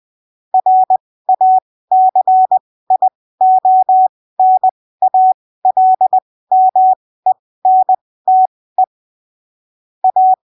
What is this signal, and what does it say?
Morse code, heard as a keyed tone, about 20 wpm, spaced as for 15 wpm, 750 Hz. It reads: RACIONALMENTE A
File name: arquivos / texto / texto 06 20ppm